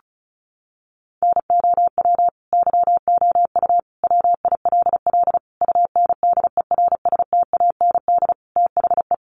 Morse code, heard as a keyed tone, about 34 wpm, 700 Hz. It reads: NOW YOU WILL UNDERSTAND THE
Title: Sentences from Top 1200 Words 35wpm